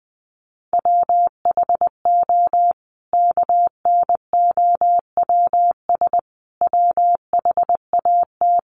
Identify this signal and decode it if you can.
Morse code, heard as a keyed tone, about 20 wpm, 700 Hz. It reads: WHO KNOWS WHAT
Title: A Sound Of Thunder 20wpm